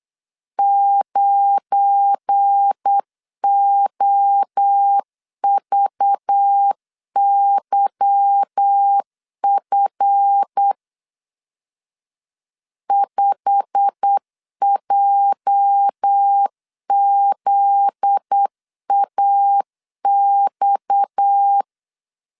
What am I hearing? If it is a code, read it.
Morse code, heard as a keyed tone, about 8 wpm, 800 Hz. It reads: 9OVYF 5JZAX
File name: LEZIONI / 8 YJ45 / 8 YJ45 8wpm 3214